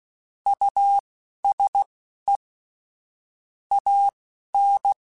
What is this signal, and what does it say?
Morse code, heard as a keyed tone, about 16 wpm, spaced as for 11 wpm, 800 Hz. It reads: USE AN
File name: Technical Subjects / Technical Subjects-mp3s / Sw Pwr Sup @10